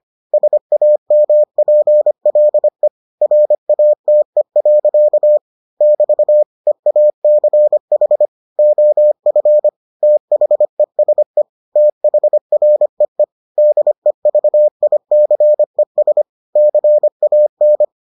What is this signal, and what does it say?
Morse code, heard as a keyed tone, about 25 wpm, 600 Hz. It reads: SAMPLE RATE. <BT> EACH OF THESE THREE DEVICES CAN